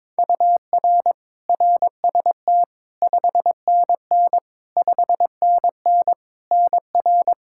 Morse code, heard as a keyed tone, about 22 wpm, 700 Hz. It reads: UR RST 5NN 5NN NR
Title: K1USN SST QSO Practice 22wpm